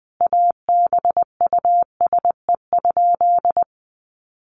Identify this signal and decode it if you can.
Morse code, heard as a keyed tone, about 20 wpm, 700 Hz. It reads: ABUSE?
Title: Sentences from Top 1300 Words 20wpm